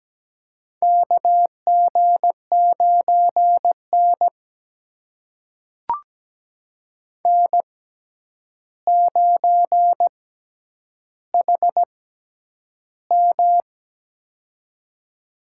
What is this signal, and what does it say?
Morse code, heard as a keyed tone, about 17 wpm, spaced as for 9 wpm, 700 Hz. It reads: KG9N N9HM
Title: USA Call Signs - 6x Character Spacing - 17wpm